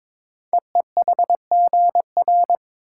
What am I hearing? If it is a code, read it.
Morse code, heard as a keyed tone, about 22 wpm, 700 Hz. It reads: EEHGR
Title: Code Groups - 5 Letters - 22wpm